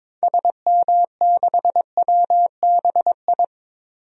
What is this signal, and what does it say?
Morse code, heard as a keyed tone, about 22 wpm, 700 Hz. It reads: SM6WBI
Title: DXCC-100 Call Signs - 6 Characters - Repeated 2x - 22wpm